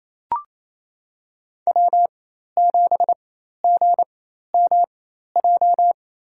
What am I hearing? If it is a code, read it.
Morse code, heard as a keyed tone, about 28 wpm, spaced as for 13 wpm, 700 Hz. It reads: W7GMJ
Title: USA Call Signs - 5 Characters - 4x Character Spacing - 28wpm